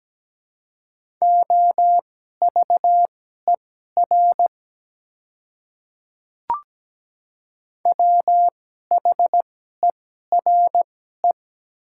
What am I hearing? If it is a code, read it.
Morse code, heard as a keyed tone, about 17 wpm, spaced as for 12 wpm, 700 Hz. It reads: OVER WHERE
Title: 2x Character Spacing - Top 200 Words - 17wpm